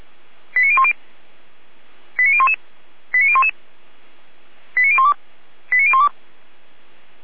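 Five-tone selective call (ZVEI), heard as quick runs of five tones, tones of about 70 ms. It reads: 89010 89011 89011 89012 89012
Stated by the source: Five-tone ZVEI selective call received by SDR at feuerwehr-orlamuende.org (Orlamünde fire brigade).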